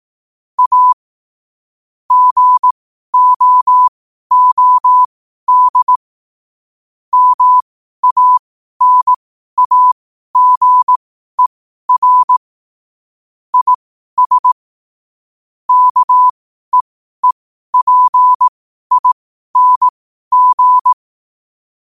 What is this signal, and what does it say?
Morse code, heard as a keyed tone, about 18 wpm, spaced as for 12 wpm, 1000 Hz. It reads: A GOOD MANAGER IS KEEPING
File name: qotdfiles / 12wpm